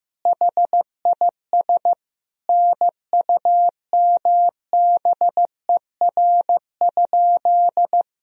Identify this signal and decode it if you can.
Morse code, heard as a keyed tone, about 15 wpm, 700 Hz. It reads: HIS NUMBER?
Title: Sentences from Top 1000 Words - Review Part 1 - 15wpm